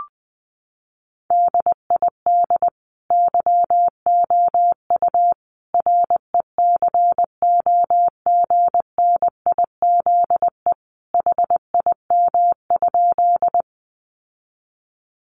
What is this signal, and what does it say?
Morse code, heard as a keyed tone, about 20 wpm, 700 Hz. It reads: DID YOU RECOGNIZE HIM?
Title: Sentences from Top 1400 Words 20wpm